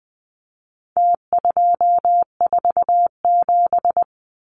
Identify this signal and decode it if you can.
Morse code, heard as a keyed tone, about 20 wpm, 700 Hz. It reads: T247
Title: Numbers - Contest Style - 4-Digit 20wpm